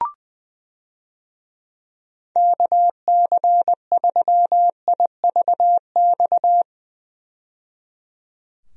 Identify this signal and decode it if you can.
Morse code, heard as a keyed tone, about 20 wpm, 700 Hz. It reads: KC3IVX
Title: USA Call Signs - 6 Characters - 20wpm